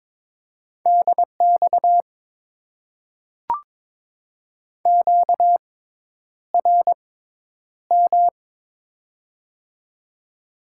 Morse code, heard as a keyed tone, about 22 wpm, spaced as for 10 wpm, 700 Hz. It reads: DX QRM
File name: Top 90 QSO Elements - 6x Character Spacing - 22wpm